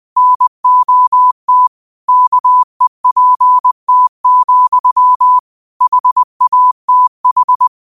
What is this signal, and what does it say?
Morse code, heard as a keyed tone, about 20 wpm, 1000 Hz. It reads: NOT KEPT, HATH
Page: New quotes every day in morse code at 20 Words per minute.